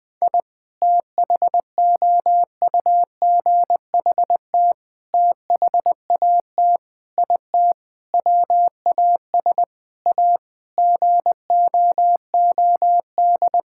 Morse code, heard as a keyed tone, about 20 wpm, 700 Hz. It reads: I THOUGHT THAT IT WAS A GOOD
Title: Sentences with 4-Word Phrases - 20wpm